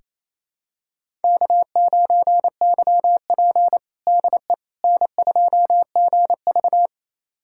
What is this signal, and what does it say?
Morse code, heard as a keyed tone, about 27 wpm, 700 Hz. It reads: K9YP DE N2GV
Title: USA Callsign DE Callsign - 4 Characters - 28wpm